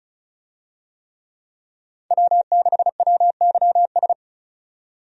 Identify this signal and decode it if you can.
Morse code, heard as a keyed tone, about 36 wpm, 700 Hz. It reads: W6WYS
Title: USA Call Signs - 5 Characters - 35wpm